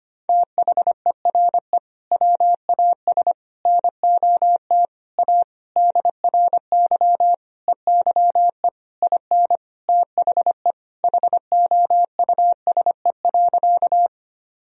Morse code, heard as a keyed tone, about 25 wpm, 700 Hz. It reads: THERE WAS NOT A DRY EYE IN THE HOUSE.
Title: Sentences from Top 1000 Words - Review Part 3 - 25wpm